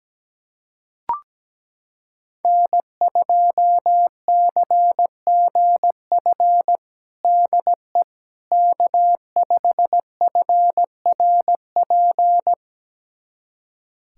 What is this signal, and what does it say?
Morse code, heard as a keyed tone, about 17 wpm, 700 Hz. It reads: N2CGF DE K5FRP
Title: USA Callsign DE Callsign - 5 Characters - 17wpm